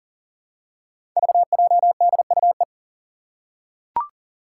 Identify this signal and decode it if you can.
Morse code, heard as a keyed tone, about 40 wpm, 690 Hz. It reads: VJDUE
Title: Five Challenging Letter-Number Combinations 40wpm